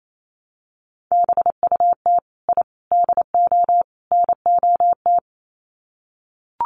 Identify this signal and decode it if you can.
Morse code, heard as a keyed tone, about 28 wpm, 700 Hz. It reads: BUT I DO NOT
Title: Sets of 4 Words 28wpm